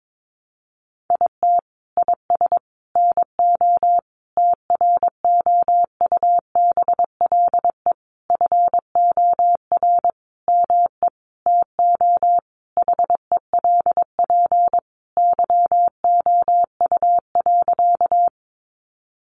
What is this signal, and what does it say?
Morse code, heard as a keyed tone, about 22 wpm, 700 Hz. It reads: IT IS NO TROUBLE FOR ME TO HELP YOU.
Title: Sentences from Top 1200 Words 22wpm